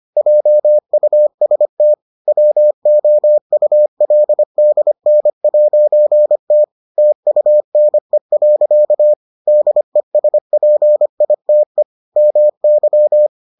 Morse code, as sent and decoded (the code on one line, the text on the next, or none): .--- ..- ... - / .-- --- ..- .-.. -.. -. .----. - / - ..- -. .-.-.- / -.. ... .--. .. - . / -- -.--
JUST WOULDN'T TUNE. DESPITE MY